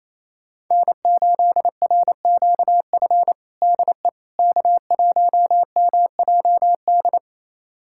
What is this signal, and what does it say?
Morse code, heard as a keyed tone, about 28 wpm, 700 Hz. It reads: N8RQF DE K1MJD
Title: USA Callsign DE Callsign - 5 Characters - 28wpm